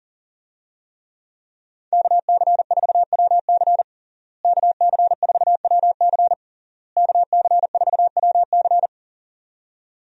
Morse code, heard as a keyed tone, about 37 wpm, 700 Hz. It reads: KC4WC KC4WC KC4WC
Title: USA Call Signs - 5 Characters - Repeated 3x - 40wpm